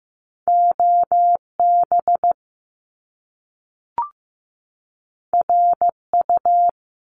Morse code, heard as a keyed tone, about 15 wpm, 700 Hz. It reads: OB RU
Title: Top Two Letter Combinations in Words 15wpm